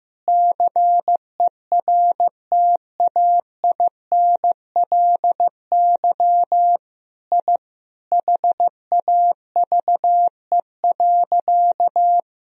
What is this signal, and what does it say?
Morse code, heard as a keyed tone, about 15 wpm, 700 Hz. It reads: CERTAINLY I HAVE.
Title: Sentences from Top 1000 Words - Review Part 1 - 15wpm